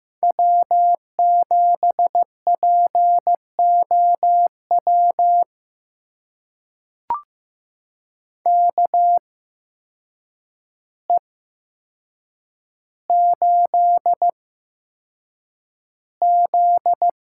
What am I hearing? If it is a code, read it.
Morse code, heard as a keyed tone, about 15 wpm, spaced as for 7 wpm, 700 Hz. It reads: W7POW KE8Z
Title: USA Call Signs - 8x Character Spacing - 15wpm